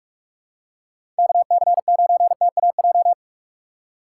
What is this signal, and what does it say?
Morse code, heard as a keyed tone, about 46 wpm, 700 Hz. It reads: KC9TAJ
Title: USA Call Signs - 3x Character Spacing - 45wpm